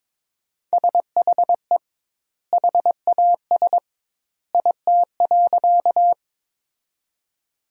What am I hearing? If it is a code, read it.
Morse code, heard as a keyed tone, about 22 wpm, 700 Hz. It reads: SHE HAS IT.
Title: 2x Word-Spacing - Sentences from Top 100 Words - 22wpm